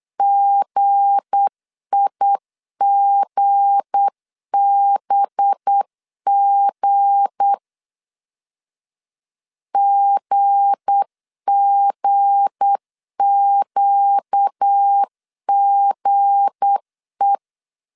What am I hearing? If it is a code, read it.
Morse code, heard as a keyed tone, about 8 wpm, 800 Hz. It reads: GIGBG GGQGE
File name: LEZIONI / 7 GF9Z / 7 G 8wpm 3118